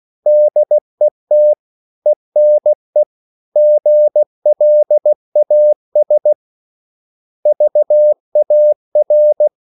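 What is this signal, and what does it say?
Morse code, heard as a keyed tone, about 16 wpm, 600 Hz. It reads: DET ENE GLAS VAR
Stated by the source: Slow (QRS) Morse code at 16 wpm, from CW med Gnister.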